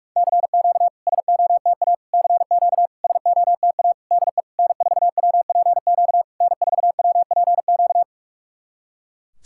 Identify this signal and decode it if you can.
Morse code, heard as a keyed tone, about 46 wpm, 690 Hz. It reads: CQ SOTA CQ SOTA DE N4WPQ N4WPQ
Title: SOTA QSO Practice 45wpm